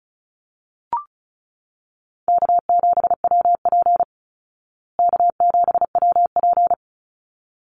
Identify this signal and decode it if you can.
Morse code, heard as a keyed tone, about 34 wpm, 700 Hz. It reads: K7WP K7WP
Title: Easy Call Signs Repeated 2x - 35wpm